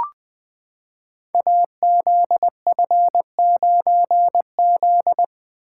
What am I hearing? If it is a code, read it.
Morse code, heard as a keyed tone, about 20 wpm, 700 Hz. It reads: AZF9Z
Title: Code Groups - 5 Letter:Numbers - 20wpm